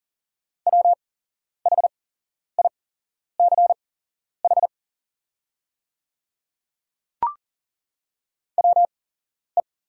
Morse code, heard as a keyed tone, about 40 wpm, spaced as for 11 wpm, 700 Hz. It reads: WHICH WE
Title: 8x Character Spacing - Top 100 Words - 40wpm